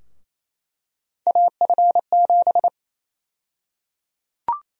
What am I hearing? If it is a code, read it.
Morse code, heard as a keyed tone, about 28 wpm, 700 Hz. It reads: AF7
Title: USA Call Sign Prefixes - Repeated 2x - 28wpm